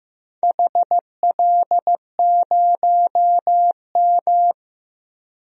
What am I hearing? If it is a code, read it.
Morse code, heard as a keyed tone, about 15 wpm, 700 Hz. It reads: HL0M
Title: DXCC-100 Call Signs - 4 Characters - Repeated 3x - 15wpm